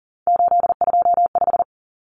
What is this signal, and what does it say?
Morse code, heard as a keyed tone, about 40 wpm, 700 Hz. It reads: 825
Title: Common Cut Numbers - 3-Digit 40wpm